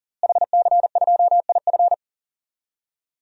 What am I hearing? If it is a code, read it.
Morse code, heard as a keyed tone, about 40 wpm, 700 Hz. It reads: HC2IF
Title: DXCC-100 Call Signs - 5 Characters - Repeated 2x - 40wpm